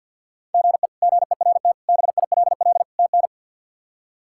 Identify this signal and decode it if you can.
Morse code, heard as a keyed tone, about 45 wpm, 700 Hz. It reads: GE GERT BILL TN
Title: K1USN SST QSO Practice - 50wpm